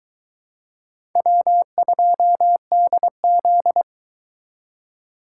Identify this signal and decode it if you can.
Morse code, heard as a keyed tone, about 22 wpm, 700 Hz. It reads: W2DZ